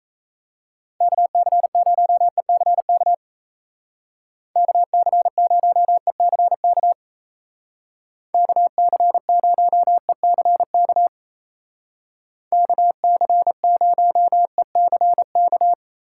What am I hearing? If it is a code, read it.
Morse code, heard as a keyed tone, about 33 wpm, 700 Hz. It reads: KC0ECK KC0ECK KC0ECK KC0ECK
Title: Speed Racing - USA 2x3 Call Signs 28wpm